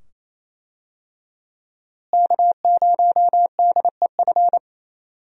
Morse code, heard as a keyed tone, about 28 wpm, 700 Hz. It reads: K0DEF
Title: USA Call Signs - 5 Characters - Repeated 2x - 28wpm